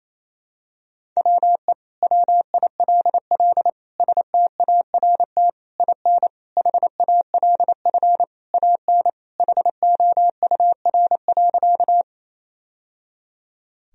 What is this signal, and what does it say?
Morse code, heard as a keyed tone, about 28 wpm, 700 Hz. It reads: WE WILL START IN HALF AN HOUR.
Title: Sentences from Top 1000 Words - Review Part 1 - 28wpm